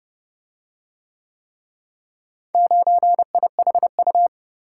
Morse code, heard as a keyed tone, about 31 wpm, 700 Hz. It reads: 9IHU